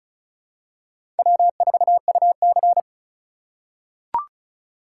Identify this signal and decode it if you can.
Morse code, heard as a keyed tone, about 34 wpm, 700 Hz. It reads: W4UC
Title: USA Call Signs - 4x Character Spacing - 35wpm